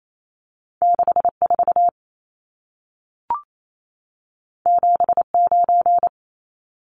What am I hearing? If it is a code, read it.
Morse code, heard as a keyed tone, about 28 wpm, 700 Hz. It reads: 64 79
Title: Numbers - 2-Digit 28wpm